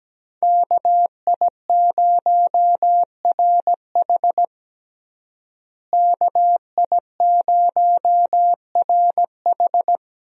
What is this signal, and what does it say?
Morse code, heard as a keyed tone, about 17 wpm, 700 Hz. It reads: KI0RH KI0RH
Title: USA Call Signs - 5 Characters - Repeated 2x - 17wpm